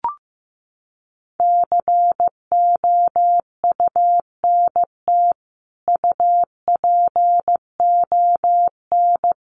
Morse code, heard as a keyed tone, about 15 wpm, 700 Hz. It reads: COUNT UPON
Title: Sentences from Top 1500 Words 15wpm